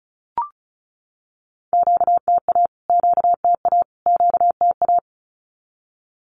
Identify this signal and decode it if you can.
Morse code, heard as a keyed tone, about 34 wpm, 700 Hz. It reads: QTA QTA QTA
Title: Q-Codes Repeated 3x 35wpm